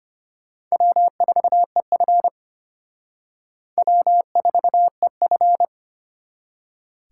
Morse code, heard as a keyed tone, about 33 wpm, 700 Hz. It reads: W4EF W4EF